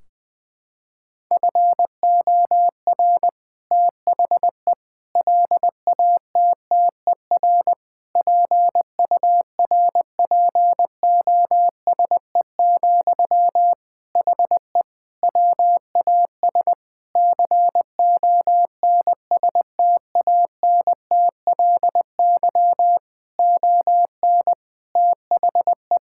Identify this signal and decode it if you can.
Morse code, heard as a keyed tone, about 20 wpm, 700 Hz. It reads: FOR THE LATTER PURPOSE, HE WAS CONSTANTLY ON THE
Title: Sentences from Top 1600 Words 20wpm